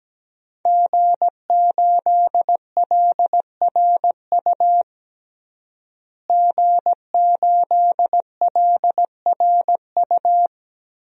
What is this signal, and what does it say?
Morse code, heard as a keyed tone, about 17 wpm, 700 Hz. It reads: G8LRU G8LRU
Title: DXCC-100 Call Signs - 5 Characters - Repeated 3x - 17wpm